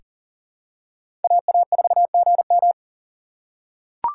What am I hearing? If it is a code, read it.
Morse code, heard as a keyed tone, about 40 wpm, 700 Hz. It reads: AA4GM
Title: USA Call Signs - 6x Character Spacing - 40wpm